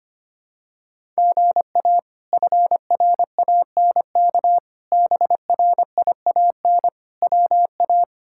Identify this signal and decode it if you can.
Morse code, heard as a keyed tone, about 25 wpm, 700 Hz. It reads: GA FRANK BRIAN WA
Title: K1USN SST QSO Practice - 25wpm